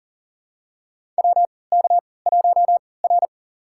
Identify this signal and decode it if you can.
Morse code, heard as a keyed tone, about 38 wpm, spaced as for 22 wpm, 700 Hz. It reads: WK1R
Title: USA Call Signs - 3x Character Spacing - 40wpm